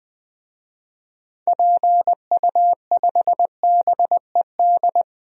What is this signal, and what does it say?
Morse code, heard as a keyed tone, about 20 wpm, 700 Hz. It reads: PU5BED
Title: DXCC-100 Call Signs - 6 Characters - 20wpm